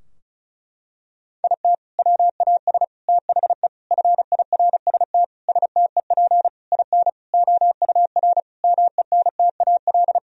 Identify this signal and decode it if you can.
Morse code, heard as a keyed tone, about 36 wpm, 700 Hz. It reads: IT WAS THE FIRST STEP IN OUR MENTAL
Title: Sentences from Top 1000 Words - Review Part 4 - 35wpm